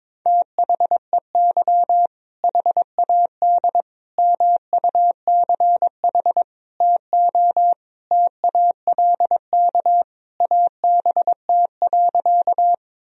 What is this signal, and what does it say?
Morse code, heard as a keyed tone, about 22 wpm, 700 Hz. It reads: THEY HAD MUCH TO TALK ABT.
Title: Sentences from Top 400 Words and Abbreviations 22wpm